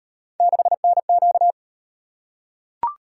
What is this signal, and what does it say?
Morse code, heard as a keyed tone, about 38 wpm, 700 Hz. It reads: 6NQ